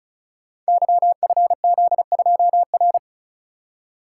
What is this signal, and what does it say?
Morse code, heard as a keyed tone, about 36 wpm, 700 Hz. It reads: YFZ2R
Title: Code Groups - 5 Letter:Numbers - 35wpm